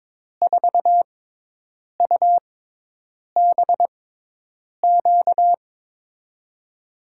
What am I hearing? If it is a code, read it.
Morse code, heard as a keyed tone, about 22 wpm, spaced as for 8 wpm, 700 Hz. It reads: 4UBQ
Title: USA Call Signs - 6 Characters - 6x Character Spacing - 22wpm